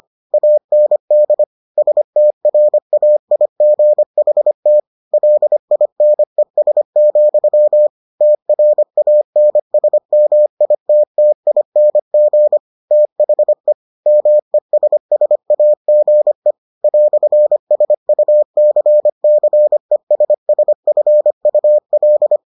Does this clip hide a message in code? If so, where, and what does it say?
Yes: Morse code from 0.3 to 22.4 s, heard as a keyed tone, about 25 wpm, 600 Hz. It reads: AND STRAIGHT LINES, TRANSMITTING THE MESSAGE "SUCCESSFUL